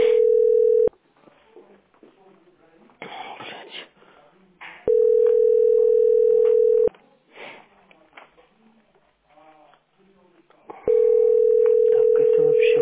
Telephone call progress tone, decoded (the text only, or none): ringback tone